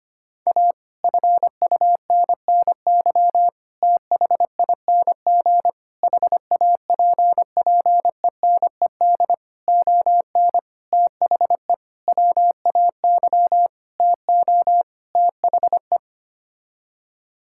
Morse code, heard as a keyed tone, about 25 wpm, 700 Hz. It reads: A FUNNY THING HAPPENED ON THE WAY TO THE
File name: Sets of 9 Words - 25wpm